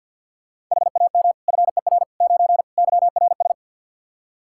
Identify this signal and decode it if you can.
Morse code, heard as a keyed tone, about 52 wpm, 700 Hz. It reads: HAM FER 9 YRS
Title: K1USN SST QSO Practice 50wpm